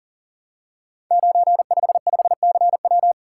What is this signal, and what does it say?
Morse code, heard as a keyed tone, about 42 wpm, 690 Hz. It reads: 955CW
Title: Code Groups - 5 Letter:Numbers - 40wpm